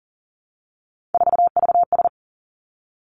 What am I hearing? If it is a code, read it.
Morse code, heard as a keyed tone, about 38 wpm, 690 Hz. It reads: 4VS
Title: Three Challenging Letter-Number Combinations 40wpm